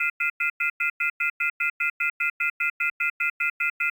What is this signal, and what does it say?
Off-hook warning tone, 1400 + 2060 + 2450 + 2600 Hz, with noise.
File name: shared / SoundPhoneNoLine